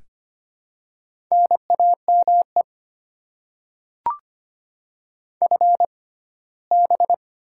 Morse code, heard as a keyed tone, about 25 wpm, spaced as for 17 wpm, 700 Hz. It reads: NAME FB
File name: Top 20 QSO Elements - 6x Character Spacing - 25wpm